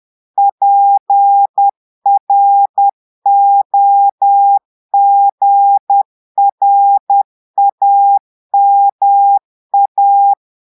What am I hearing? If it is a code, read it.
Morse code, heard as a keyed tone, about 10 wpm, 800 Hz. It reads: PROGRAMA
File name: portugais texte en clair 2 vitesse 10 mots minutes